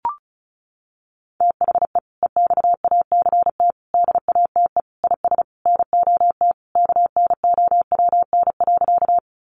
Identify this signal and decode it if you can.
Morse code, heard as a keyed tone, about 34 wpm, 700 Hz. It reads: THE EXACT DATE IS NOT KNOWN.
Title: Sentences from Top 1300 Words 35wpm